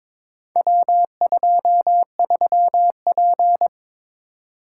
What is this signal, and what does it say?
Morse code, heard as a keyed tone, about 22 wpm, 700 Hz. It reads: W23P
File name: practice / 22wpm / Four Letter-Number Combinations - 22wpm